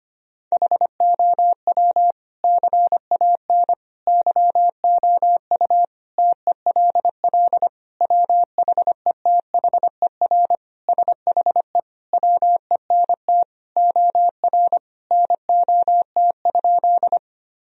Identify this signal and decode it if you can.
Morse code, heard as a keyed tone, about 25 wpm, 700 Hz. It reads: HOW CAN YOU TELL WHETHER SHE WENT OR NOT?
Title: Sentences from Top 1000 Words - Review Part 3 - 25wpm